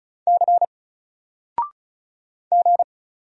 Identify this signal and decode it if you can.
Morse code, heard as a keyed tone, about 35 wpm, 700 Hz. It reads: C G
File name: Single-Letter-Number-35wpm